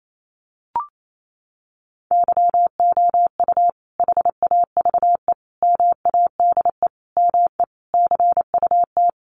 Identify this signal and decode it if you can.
Morse code, heard as a keyed tone, about 28 wpm, 700 Hz. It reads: YOU HAVE MADE ME CUT
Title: Sentences from Top 1300 Words 28wpm